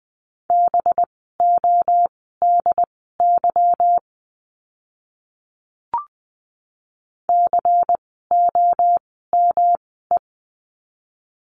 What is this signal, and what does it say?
Morse code, heard as a keyed tone, about 20 wpm, spaced as for 14 wpm, 700 Hz. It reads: BODY COME
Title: 185 Lesson 30f - Letter b - Words 20wpm